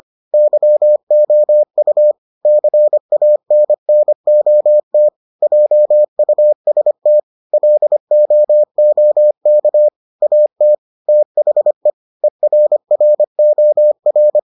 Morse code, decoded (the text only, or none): YOU CANNOT JUST LOOK AT THE ERROR